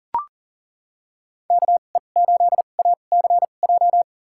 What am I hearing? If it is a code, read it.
Morse code, heard as a keyed tone, about 40 wpm, spaced as for 29 wpm, 700 Hz. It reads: KE8ACJ
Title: USA Call Signs - 6 Characters - 2x Character Spacing - 40wpm